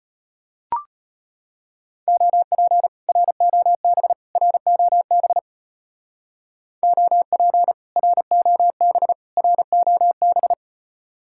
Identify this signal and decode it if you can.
Morse code, heard as a keyed tone, about 30 wpm, 700 Hz. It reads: OP ROB ROB OP ROB ROB
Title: Speed-Racing Common QSO Phrases - 25wpm